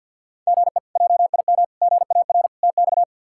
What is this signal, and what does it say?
Morse code, heard as a keyed tone, about 52 wpm, 690 Hz. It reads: GE JIM GAR TX
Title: K1USN SST QSO Practice - 50wpm